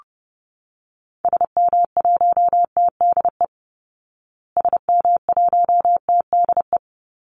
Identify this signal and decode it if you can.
Morse code, heard as a keyed tone, about 30 wpm, 700 Hz. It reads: SM1TDE SM1TDE
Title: Call Signs Repeated 3x - 30wpm